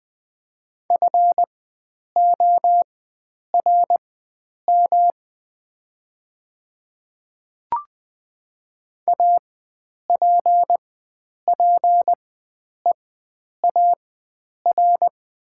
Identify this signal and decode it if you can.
Morse code, heard as a keyed tone, about 20 wpm, spaced as for 9 wpm, 700 Hz. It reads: FORM APPEAR
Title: 4x Character Spacing - Top 500 Words - 20wpm